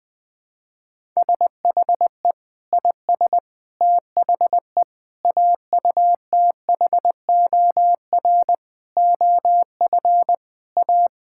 Morse code, heard as a keyed tone, about 20 wpm, 700 Hz. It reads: SHE IS THE AUTHOR OF A
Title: Sentences with 4-Word Phrases - 20wpm